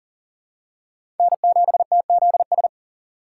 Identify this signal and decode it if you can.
Morse code, heard as a keyed tone, about 40 wpm, 700 Hz. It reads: N7TZS
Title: USA Call Signs - 5 Characters - 2x Character Spacing - 40wpm